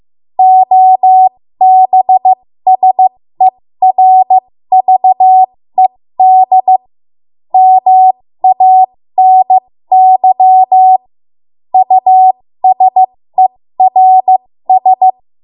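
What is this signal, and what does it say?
Morse code, heard as a keyed tone, about 15 wpm, spaced as for 13 wpm, 750 Hz. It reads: OBSERVED MANY USERS
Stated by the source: 13 wpm Morse code practice archive transmission.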